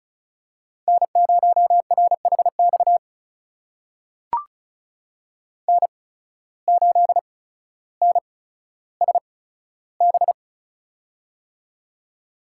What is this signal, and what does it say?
Morse code, heard as a keyed tone, about 36 wpm, spaced as for 15 wpm, 700 Hz. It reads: N0RHX N8NSB
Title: USA Call Signs - 5 Characters - 8x Character Spacing - 35wpm